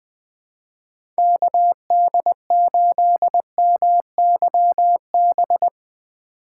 Morse code, heard as a keyed tone, about 20 wpm, 700 Hz. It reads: KD8MYB
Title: USA Call Signs - 6 Characters - 2x Character Spacing - 20wpm